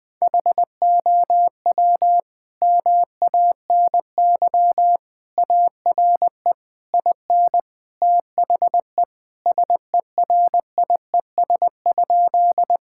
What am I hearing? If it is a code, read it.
Morse code, heard as a keyed tone, about 20 wpm, 700 Hz. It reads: HOW MANY ARE IN THE SERIES?
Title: Sentences from Top 1000 Words - Review Part 5 - 20wpm